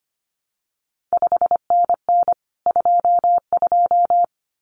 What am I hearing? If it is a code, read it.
Morse code, heard as a keyed tone, about 25 wpm, 700 Hz. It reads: 5NN 22
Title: CQ WW DX Contest Exchange Encore - 25wpm